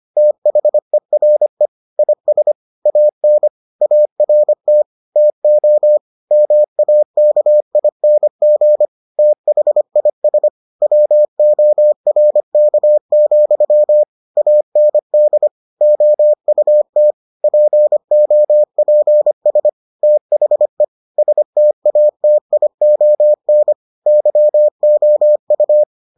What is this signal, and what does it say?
Morse code, heard as a keyed tone, about 25 wpm, 600 Hz. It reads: THERE IS AN ART TO MAKING THIS WORK, AND OUT POPS THE STATION YOU